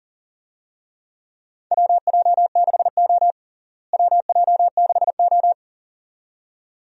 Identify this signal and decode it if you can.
Morse code, heard as a keyed tone, about 40 wpm, 700 Hz. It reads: WJ6O WJ6O
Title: USA Call Signs - 4 Characters - Repeated 2x - 40wpm